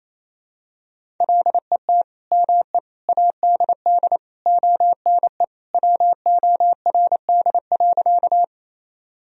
Morse code, heard as a keyed tone, about 28 wpm, 700 Hz. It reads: LET ME ADD ONE WORD.
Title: Sentences from Top 1000 Words - Review Part 1 - 28wpm